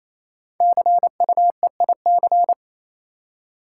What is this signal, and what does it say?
Morse code, heard as a keyed tone, about 27 wpm, 700 Hz. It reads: CUEIC
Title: Code Groups - 5 Letter:Numbers - 28wpm